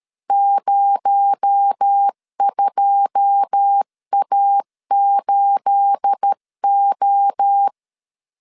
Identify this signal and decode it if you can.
Morse code, heard as a keyed tone, about 13 wpm, 800 Hz. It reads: 02A8O
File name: LEZIONI / 9 28 12wpm 2351